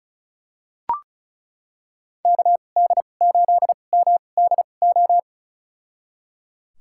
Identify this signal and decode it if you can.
Morse code, heard as a keyed tone, about 36 wpm, spaced as for 25 wpm, 700 Hz. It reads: KD8MDO
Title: USA Call Signs - 6 Characters - 2x Character Spacing - 35wpm